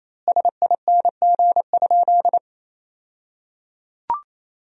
Morse code, heard as a keyed tone, about 27 wpm, 700 Hz. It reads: SING?
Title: 2x Word-Spacing - Sentences from Top 400 Words - 28wpm